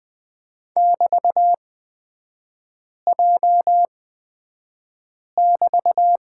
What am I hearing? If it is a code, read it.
Morse code, heard as a keyed tone, about 20 wpm, spaced as for 5 wpm, 700 Hz. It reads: <BT>J<BT>
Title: 224 Lesson 36e - BT - Three Characters 20wpm